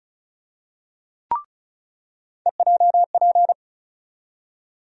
Morse code, heard as a keyed tone, about 36 wpm, 700 Hz. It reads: EJP
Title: USA Call Sign Postfixes - 3 Letters 35wpm